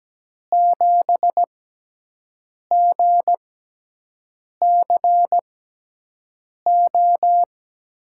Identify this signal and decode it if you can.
Morse code, heard as a keyed tone, about 17 wpm, spaced as for 6 wpm, 700 Hz. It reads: 7GCO
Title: USA Call Signs - 6x Character Spacing - 17wpm